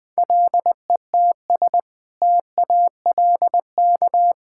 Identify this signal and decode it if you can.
Morse code, heard as a keyed tone, about 19 wpm, 700 Hz. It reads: LETS TALK
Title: Sentences with 2-Word Phrases - 20wpm